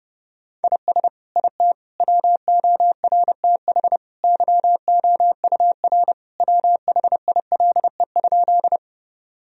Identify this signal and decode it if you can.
Morse code, heard as a keyed tone, about 29 wpm, 700 Hz. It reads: IS IT WORTH YOUR WHILE?
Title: Sentences from Top 1000 Words - Review Part 1 - 30wpm